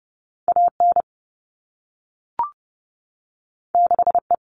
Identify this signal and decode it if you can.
Morse code, heard as a keyed tone, about 30 wpm, 700 Hz. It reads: AN 6E
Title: All Cut Numbers - 2-Digit 30wpm